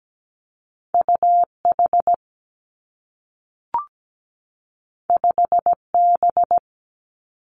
Morse code, heard as a keyed tone, about 17 wpm, 700 Hz. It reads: UH 5B
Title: Two Challenging Letter-Number Combinations 17wpm